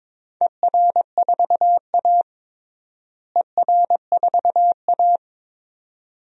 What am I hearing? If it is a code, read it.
Morse code, heard as a keyed tone, about 22 wpm, 700 Hz. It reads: ER4A ER4A
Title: DXCC-100 Call Signs - 4 Characters - Repeated 2x - 22wpm